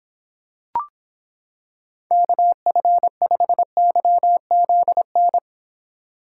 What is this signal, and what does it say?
Morse code, heard as a keyed tone, about 26 wpm, 700 Hz. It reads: KF5YZN